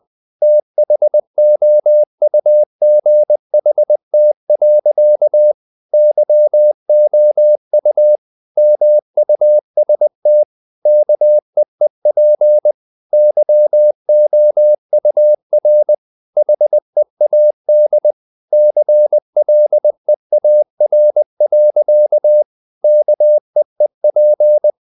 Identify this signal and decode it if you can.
Morse code, heard as a keyed tone, about 20 wpm, 600 Hz. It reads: THOUGHT. YOU MUST KEEP YOUR HEAD CLEAR. KEEP
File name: the old man and the sea part 2 20wpm